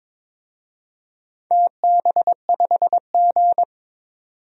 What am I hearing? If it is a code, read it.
Morse code, heard as a keyed tone, about 22 wpm, 700 Hz. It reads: TB5G
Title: DXCC-100 Call Signs - 4 Characters - 22wpm